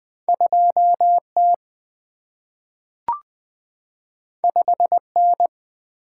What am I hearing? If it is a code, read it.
Morse code, heard as a keyed tone, about 20 wpm, 700 Hz. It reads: 2T 5N
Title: Single-Letter Single-Number 20wpm